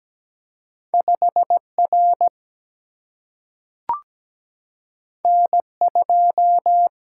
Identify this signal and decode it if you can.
Morse code, heard as a keyed tone, about 17 wpm, 700 Hz. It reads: 5 R N 2
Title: Single-Letter Single-Number 17wpm